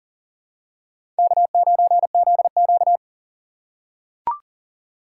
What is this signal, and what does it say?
Morse code, heard as a keyed tone, about 40 wpm, 700 Hz. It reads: K9ZQ
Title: USA Call Signs - 4 Characters - 3x Character Spacing - 40wpm